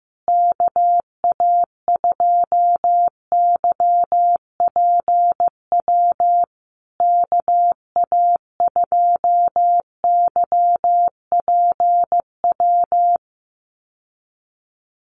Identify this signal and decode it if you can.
Morse code, heard as a keyed tone, about 15 wpm, 700 Hz. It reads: KA2YPW KA2YPW
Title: USA Call Signs Repeated 3x 15wpm